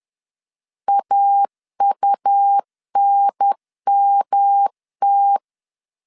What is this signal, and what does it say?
Morse code, heard as a keyed tone, about 10 wpm, 800 Hz. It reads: AUNMT
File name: CW / DET / 1 3 1500C S B 10wpm 3335